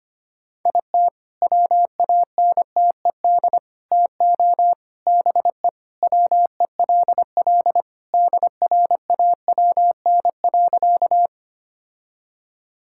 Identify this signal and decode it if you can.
Morse code, heard as a keyed tone, about 25 wpm, 700 Hz. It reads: IT WANTED TO BE WELL DRAWN.
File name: practice / 25wpm / All-Age Appropriate Jokes - 25wpm